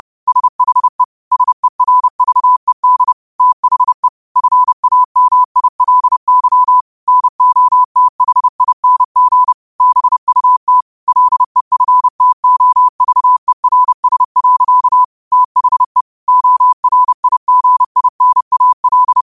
Morse code, heard as a keyed tone, about 30 wpm, 1000 Hz. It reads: SHE SERVED THE FAMILY NOTHING BUT LEFTOVERS. THE ORIGINAL
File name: qotdfiles / 30wpm